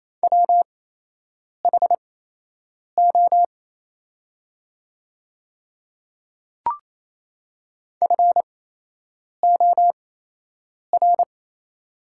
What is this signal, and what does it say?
Morse code, heard as a keyed tone, about 28 wpm, spaced as for 8 wpm, 700 Hz. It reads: WHO FOR
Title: 8x Character Spacing - Top 100 Words - 28wpm